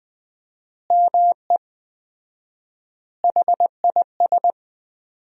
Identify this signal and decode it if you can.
Morse code, heard as a keyed tone, about 19 wpm, 700 Hz. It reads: ME HIS
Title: 4x Word-Spacing - Sentences from Top 400 Words - 20wpm